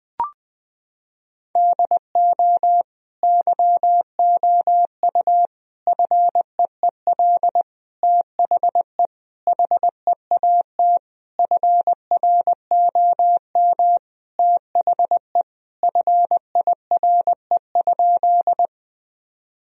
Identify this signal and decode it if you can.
Morse code, heard as a keyed tone, about 20 wpm, 700 Hz. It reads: DO YOU FEEL THE HEAT FROM THE FIRE?
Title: Sentences from Top 1000 Words - Review Part 3 - 20wpm